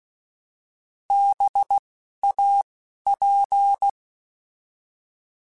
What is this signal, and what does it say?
Morse code, heard as a keyed tone, about 16 wpm, spaced as for 11 wpm, 800 Hz. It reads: BAP
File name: Alphabetical / Alphabetical mp3s / Bs @10